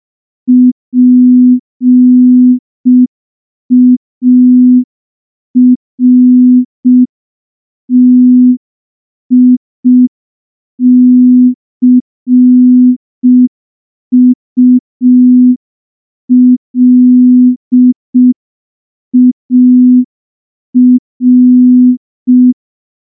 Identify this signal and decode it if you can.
Morse code, heard as a keyed tone, about 5 wpm, 250 Hz. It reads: PARTICULAR
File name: morse / morse-training-05WPM